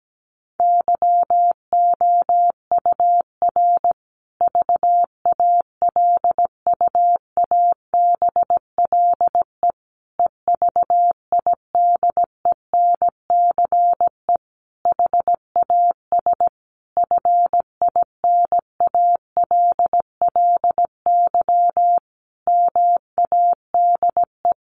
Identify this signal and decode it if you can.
Morse code, heard as a keyed tone, about 17 wpm, 700 Hz. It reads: YOUR VALUABLE EVIDENCE HAS FINALLY MADE
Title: Sentences from Top 1300 Words 17wpm